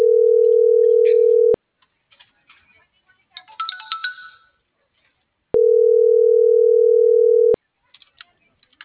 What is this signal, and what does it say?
Ringback tone (North America), 440 + 480 Hz.
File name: CENTRAL SAGOT / out xquenda.sagot